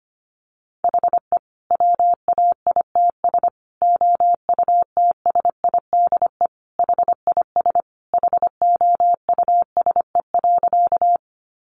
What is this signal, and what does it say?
Morse code, heard as a keyed tone, about 25 wpm, 700 Hz. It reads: HE WAITS OUTSIDE HIS HOUSE.
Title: Additional Easy Phrases 25wpm